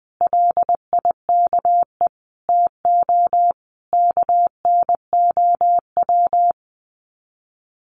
Morse code, heard as a keyed tone, about 20 wpm, 700 Hz. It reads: LIKE TO KNOW
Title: Sets of 5 Words Encore 20wpm